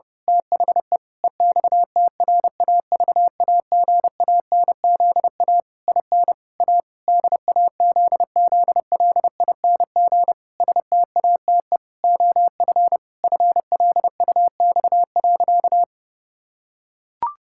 Morse code, heard as a keyed tone, about 30 wpm, 700 Hz. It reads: THE EXTRAVAGANZA IN A DAZZLING STATE OF FLUX.
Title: Pangram Challenge v2 - 30wpm